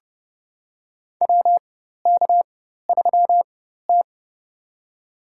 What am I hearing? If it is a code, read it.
Morse code, heard as a keyed tone, about 30 wpm, spaced as for 14 wpm, 700 Hz. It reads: WK3T
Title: USA Call Signs - 4 Characters - 4x Character Spacing - 30wpm